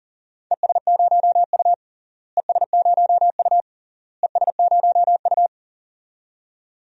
Morse code, heard as a keyed tone, about 40 wpm, 700 Hz. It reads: ES0U ES0U ES0U
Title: DXCC-100 Call Signs - 4 Characters - Repeated 3x - 40wpm